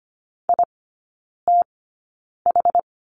Morse code, heard as a keyed tone, about 25 wpm, spaced as for 9 wpm, 700 Hz. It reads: ITH
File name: course / 25:10wpm / 093 Lesson 16e - Letter c - Three Characters 25wpm